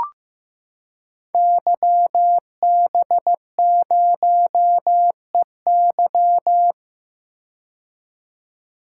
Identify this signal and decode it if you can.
Morse code, heard as a keyed tone, about 15 wpm, 700 Hz. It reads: YB0EY
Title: Code Groups - 5 Letter:Numbers - 15wpm